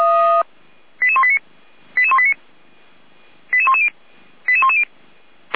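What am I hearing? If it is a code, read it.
Five-tone selective call (ZVEI), heard as quick runs of five tones, tones of about 70 ms. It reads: 88189 88189 88110 88110
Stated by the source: ZVEI five-tone selective call received on an SDR.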